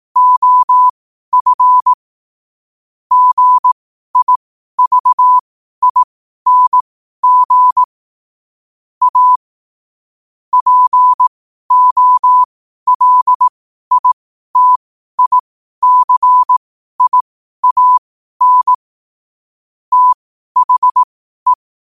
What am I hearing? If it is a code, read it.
Morse code, heard as a keyed tone, about 18 wpm, spaced as for 12 wpm, 1000 Hz. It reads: OF GIVING A POLITICIAN THE
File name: qotdfiles / 12wpm